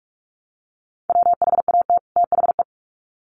Morse code, heard as a keyed tone, about 44 wpm, 700 Hz. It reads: WHAT THE
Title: Sets of 2 Words 45wpm